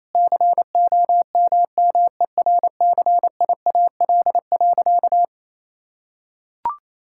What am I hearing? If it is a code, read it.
Morse code, heard as a keyed tone, about 28 wpm, 700 Hz. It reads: COMMERCIAL.
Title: Sentences from Top 1000 Words - Review Part 5 - 28wpm